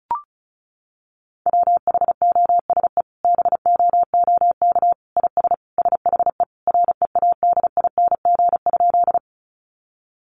Morse code, heard as a keyed tone, about 34 wpm, 700 Hz. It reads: WHOSE BOOK IS SHE READING?
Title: Sentences from Top 1200 Words 35wpm